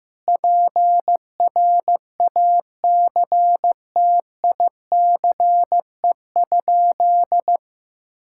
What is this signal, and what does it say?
Morse code, heard as a keyed tone, about 15 wpm, 700 Hz. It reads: PRACTICE?
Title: Sentences from Top 1000 Words - Review Part 6 - 15wpm